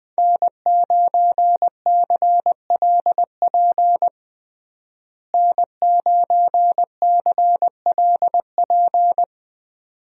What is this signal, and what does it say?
Morse code, heard as a keyed tone, about 21 wpm, 700 Hz. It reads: N9CLP N9CLP
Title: USA Call Signs - 5 Characters - Repeated 2x - 20wpm